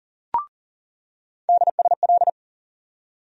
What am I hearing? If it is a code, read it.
Morse code, heard as a keyed tone, about 40 wpm, 700 Hz. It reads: DSL
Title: USA Call Sign Postfixes - 3 Letters 40wpm